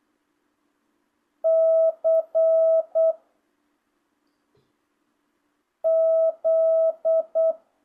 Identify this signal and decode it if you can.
Morse code, heard as a keyed tone, about 8 wpm, 660 Hz. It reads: C Z
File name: jh8cbh / 2025cw